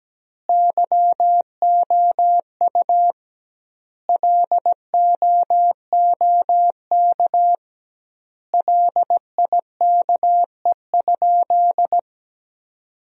Morse code, heard as a keyed tone, about 17 wpm, 700 Hz. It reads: YOU LOOK LIKE?
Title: 2x Word-Spacing - Sentences from Top 100 Words - 17wpm